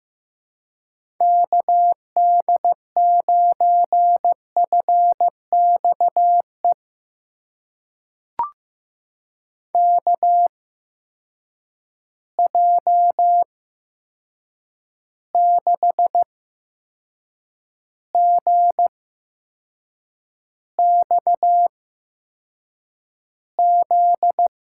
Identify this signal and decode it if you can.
Morse code, heard as a keyed tone, about 15 wpm, spaced as for 6 wpm, 700 Hz. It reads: KD9FXE KJ6GXZ
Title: USA Call Signs - 6 Characters - 8x Character Spacing - 15wpm